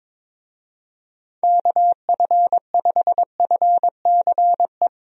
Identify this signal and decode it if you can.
Morse code, heard as a keyed tone, about 22 wpm, 700 Hz. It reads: KF5FCE
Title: USA Call Signs - 6 Characters - Repeated 2x - 22wpm